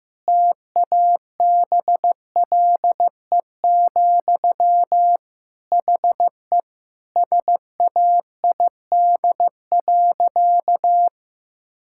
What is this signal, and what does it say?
Morse code, heard as a keyed tone, about 15 wpm, 700 Hz. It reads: TABLE, HE SAID.
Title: Sentences from Top 1000 Words - Review Part 6 - 15wpm